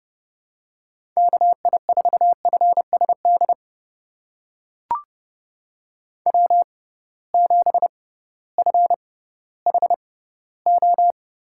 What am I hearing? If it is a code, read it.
Morse code, heard as a keyed tone, about 30 wpm, spaced as for 16 wpm, 700 Hz. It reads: KI4FSD W7FHO
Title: USA Call Signs - 6x Character Spacing - 30wpm